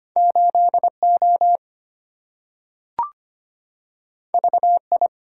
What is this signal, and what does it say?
Morse code, heard as a keyed tone, about 25 wpm, 700 Hz. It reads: 8O VI
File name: Two Letter-Number Combinations - 25wpm